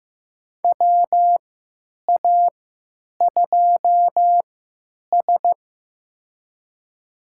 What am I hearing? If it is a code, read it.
Morse code, heard as a keyed tone, about 15 wpm, spaced as for 9 wpm, 700 Hz. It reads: WA2S
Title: USA Call Signs - 4 Characters - 3x Character Spacing - 15wpm